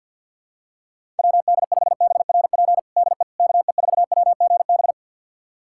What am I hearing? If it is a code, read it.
Morse code, heard as a keyed tone, about 47 wpm, 700 Hz. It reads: WD5BRP DE KE4WGB
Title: USA Callsign DE Callsign 50wpm